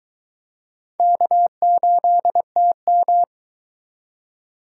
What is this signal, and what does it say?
Morse code, heard as a keyed tone, about 22 wpm, 700 Hz. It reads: K8TM